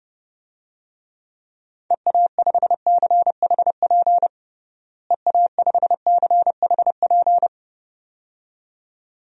Morse code, heard as a keyed tone, about 30 wpm, 700 Hz. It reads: EA5CHP EA5CHP
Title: DXCC-100 Call Signs - 6 Characters - Repeated 2x - 30wpm